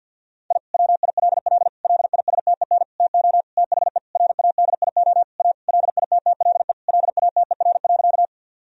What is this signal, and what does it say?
Morse code, heard as a keyed tone, about 50 wpm, 700 Hz. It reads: I WILL LISTEN TO THE RADIO A LITTLE LATER.
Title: Sentences from Top 1000 Words - Review Part 3 - 50wpm